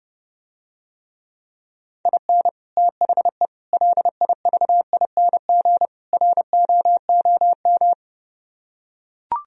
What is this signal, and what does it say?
Morse code, heard as a keyed tone, about 30 wpm, 700 Hz. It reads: IN THE LIVING ROOM